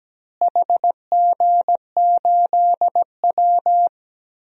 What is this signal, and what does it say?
Morse code, heard as a keyed tone, about 17 wpm, 700 Hz. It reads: HG8W
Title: DXCC-100 Call Signs - 4 Characters - Repeated 3x - 17wpm